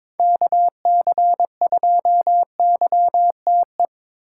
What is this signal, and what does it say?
Morse code, heard as a keyed tone, about 21 wpm, 700 Hz. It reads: KC2YTE
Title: USA Call Signs - 6 Characters - 4x Character Spacing - 22wpm